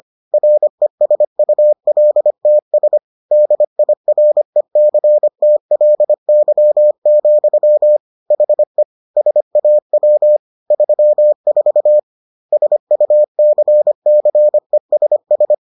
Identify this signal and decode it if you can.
Morse code, heard as a keyed tone, about 25 wpm, 600 Hz. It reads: RESULTS DIRECTLY, HE SAW 34 SUCCESS